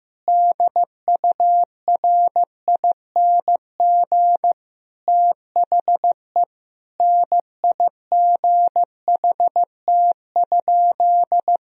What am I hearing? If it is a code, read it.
Morse code, heard as a keyed tone, about 15 wpm, 700 Hz. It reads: DURING THE NIGHT?
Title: Sentences from Top 1000 Words - Review Part 2 - 15wpm